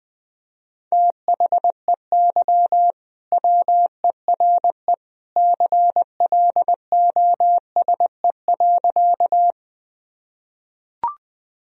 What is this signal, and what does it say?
Morse code, heard as a keyed tone, about 21 wpm, 700 Hz. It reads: THEY WERE CLOSE.
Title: Sentences from Top 1000 Words - Review Part 1 - 20wpm